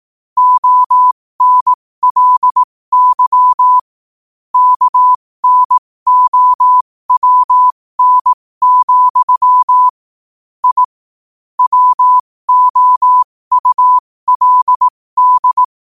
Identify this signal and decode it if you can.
Morse code, heard as a keyed tone, about 18 wpm, spaced as for 15 wpm, 1000 Hz. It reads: ONLY KNOWN, I WOULD
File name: qotdfiles / 15wpm